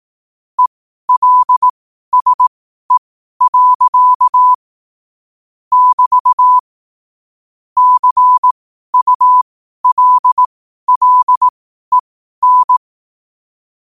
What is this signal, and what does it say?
Morse code, heard as a keyed tone, about 18 wpm, spaced as for 12 wpm, 1000 Hz. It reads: ELSE. <BT> CULLEN
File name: qotdfiles / 12wpm